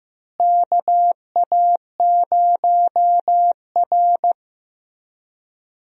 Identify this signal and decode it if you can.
Morse code, heard as a keyed tone, about 15 wpm, 700 Hz. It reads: KA0R
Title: USA Call Signs - 4 Characters - Repeated 3x - 15wpm